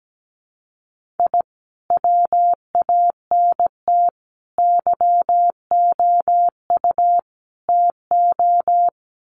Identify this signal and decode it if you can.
Morse code, heard as a keyed tone, about 17 wpm, 700 Hz. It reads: I WANT YOU TO
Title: Sets of 4 Words 17wpm